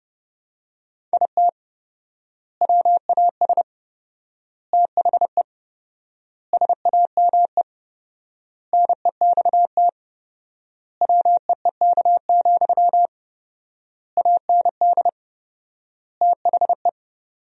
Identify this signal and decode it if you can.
Morse code, heard as a keyed tone, about 30 wpm, 700 Hz. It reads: IT WAS THE SAME NEXT WEEK, AND THE
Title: 4x Word-Spacing - Sentences from Top 500 Words - 30wpm